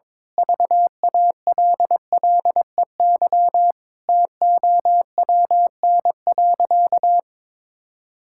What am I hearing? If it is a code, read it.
Morse code, heard as a keyed tone, about 22 wpm, 700 Hz. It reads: VALLEY TOWN.
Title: Pangram Challenge v2 - 22wpm